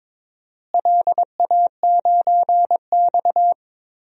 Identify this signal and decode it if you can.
Morse code, heard as a keyed tone, about 22 wpm, 700 Hz. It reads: LA9X
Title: DXCC-100 Call Signs - 4 Characters - Repeated 2x - 22wpm